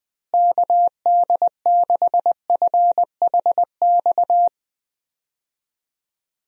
Morse code, heard as a keyed tone, about 20 wpm, 700 Hz. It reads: KD6FHX
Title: USA Call Signs - 6 Characters - Repeated 2x - 20wpm